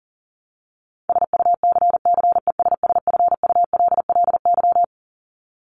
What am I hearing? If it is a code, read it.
Morse code, heard as a keyed tone, about 40 wpm, 700 Hz. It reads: SUCCESSFULLY
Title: Top 1501-1750 Words 40wpm